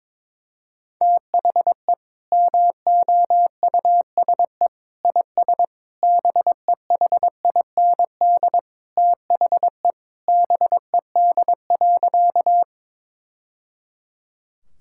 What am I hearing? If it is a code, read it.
Morse code, heard as a keyed tone, about 22 wpm, 700 Hz. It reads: THE MOUSE IS BEHIND THE BED.
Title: Sentences from Top 1000 Words - Review Part 5 - 22wpm